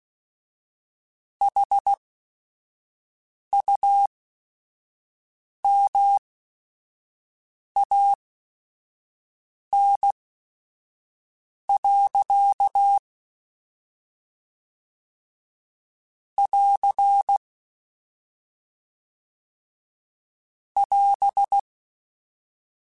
Morse code, heard as a keyed tone, about 16 wpm, spaced as for 5 wpm, 800 Hz. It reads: HUMAN. <AR> <AS>
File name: Chuckles / Chuckles-mp3s / HumanErr @5